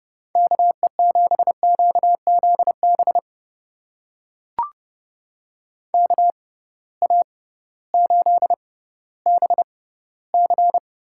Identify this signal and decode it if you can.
Morse code, heard as a keyed tone, about 30 wpm, spaced as for 16 wpm, 700 Hz. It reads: KE7QZB KA8BC
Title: USA Call Signs - 6 Characters - 6x Character Spacing - 30wpm